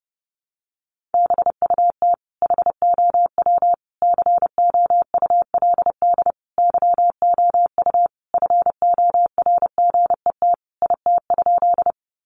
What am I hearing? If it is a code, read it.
Morse code, heard as a keyed tone, about 30 wpm, 700 Hz. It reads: BUT HOW COULD YOU FORGET IT?
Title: Sentences from Top 1200 Words 30wpm